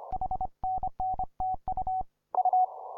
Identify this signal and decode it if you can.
Morse code, heard as a keyed tone, about 30 wpm, 740 Hz. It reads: HNNTU U
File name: simple-file-list / k5d 20m cw